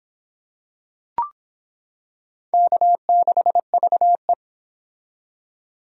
Morse code, heard as a keyed tone, about 25 wpm, 700 Hz. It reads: K6VE